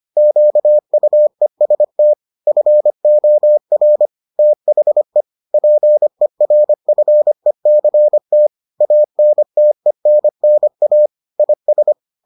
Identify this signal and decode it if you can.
Morse code, heard as a keyed tone, about 25 wpm, 600 Hz. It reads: QUEST FOR THE PERFECT ANTENNA IS